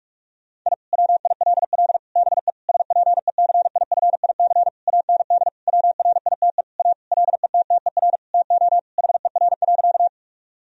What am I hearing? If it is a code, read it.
Morse code, heard as a keyed tone, about 43 wpm, 700 Hz. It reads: I WILL BE SPECIFIC AND WRITE A LETTER TO HER.
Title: Sentences from Top 1000 Words - Review Part 3 - 45wpm